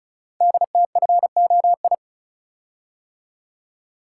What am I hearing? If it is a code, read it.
Morse code, heard as a keyed tone, about 36 wpm, 700 Hz. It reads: DTFOI
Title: Code Groups - 5 Letters - 35wpm